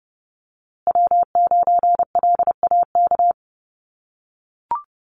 Morse code, heard as a keyed tone, about 30 wpm, 700 Hz. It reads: W9LAK
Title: USA Call Signs Repeated 3x 30wpm